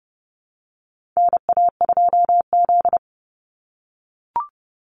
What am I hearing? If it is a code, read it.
Morse code, heard as a keyed tone, about 30 wpm, 700 Hz. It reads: NA2Z
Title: USA Call Signs 30wpm